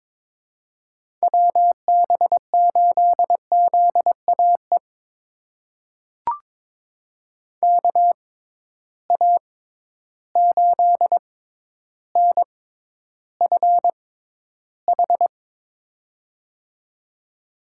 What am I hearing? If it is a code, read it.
Morse code, heard as a keyed tone, about 22 wpm, spaced as for 11 wpm, 700 Hz. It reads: WB8ZAE KA8NFH
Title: USA Call Signs - 6 Characters - 6x Character Spacing - 22wpm